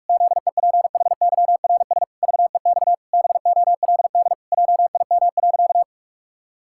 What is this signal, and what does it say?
Morse code, heard as a keyed tone, about 46 wpm, 700 Hz. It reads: ZEPHYRS VEX BOLD JIM.
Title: Pangram Challenge v2 - 45wpm